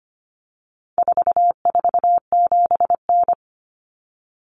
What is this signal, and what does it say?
Morse code, heard as a keyed tone, about 25 wpm, 700 Hz. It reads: 447N
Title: Common Cut Numbers - 4-Digit 25wpm